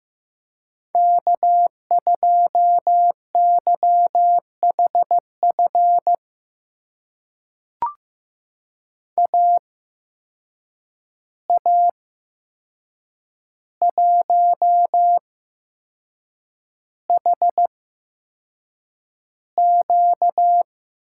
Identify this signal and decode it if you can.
Morse code, heard as a keyed tone, about 15 wpm, spaced as for 6 wpm, 700 Hz. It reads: K2YHF AA1HQ
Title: USA Call Signs - 5 Characters - 8x Character Spacing - 15wpm